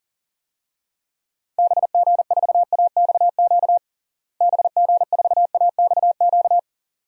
Morse code, heard as a keyed tone, about 40 wpm, 700 Hz. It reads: BG4AXQ BG4AXQ
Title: DXCC-100 Call Signs - 6 Characters - Repeated 2x - 40wpm